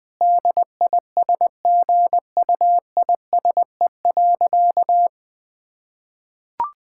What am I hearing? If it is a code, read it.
Morse code, heard as a keyed tone, about 20 wpm, 700 Hz. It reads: DISGUISE.
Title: Difficult Sentences with Hints - 20wpm